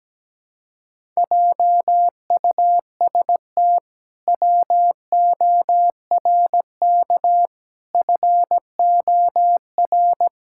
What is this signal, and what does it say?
Morse code, heard as a keyed tone, about 17 wpm, 700 Hz. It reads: JUST WORK FOR
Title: Pangram Challenge v2 - 17wpm